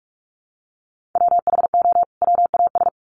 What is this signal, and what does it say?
Morse code, heard as a keyed tone, about 44 wpm, 700 Hz. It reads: WHO WAS
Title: Sets of 2 Words 45wpm